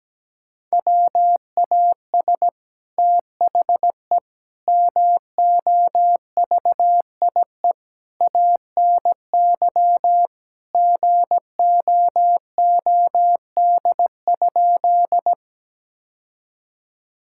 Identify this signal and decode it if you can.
Morse code, heard as a keyed tone, about 17 wpm, 700 Hz. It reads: WAS THE MOVIE ANY GOOD?
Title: Sentences from Top 1000 Words - Review Part 5 - 17wpm